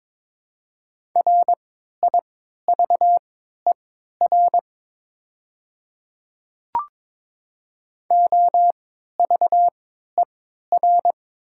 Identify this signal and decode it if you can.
Morse code, heard as a keyed tone, about 22 wpm, spaced as for 13 wpm, 700 Hz. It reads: RIVER OVER
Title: 3x Character Spacing - Top 500 Words - 22wpm